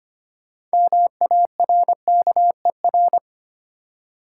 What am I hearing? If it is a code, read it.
Morse code, heard as a keyed tone, about 25 wpm, 700 Hz. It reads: MARKER
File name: practice / 25wpm / Easy Words with Hints - 25wpm